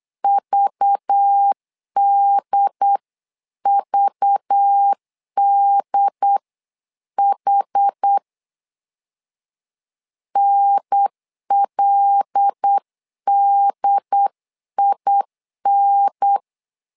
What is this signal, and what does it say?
Morse code, heard as a keyed tone, about 8 wpm, 800 Hz. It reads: VDVDH NLDIN
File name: LEZIONI / 5 LBVC / 5 LBV 8wpm 2837